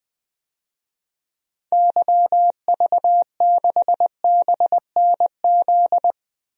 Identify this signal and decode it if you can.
Morse code, heard as a keyed tone, about 20 wpm, 700 Hz. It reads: YV6BNZ
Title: DXCC-100 Call Signs - 6 Characters - 20wpm